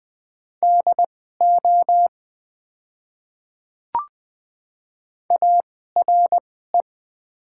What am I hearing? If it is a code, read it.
Morse code, heard as a keyed tone, about 20 wpm, spaced as for 15 wpm, 700 Hz. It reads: DO ARE
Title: 2x Character Spacing - Top 100 Words - 20wpm